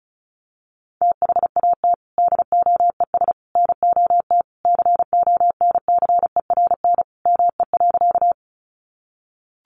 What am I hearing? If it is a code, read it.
Morse code, heard as a keyed tone, about 34 wpm, 700 Hz. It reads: THAT DOES NOT CONCERN ME.
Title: Sentences from Top 1000 Words 35wpm